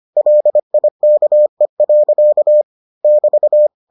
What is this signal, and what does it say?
Morse code, heard as a keyed tone, about 25 wpm, 600 Hz. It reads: LIKE. <BT>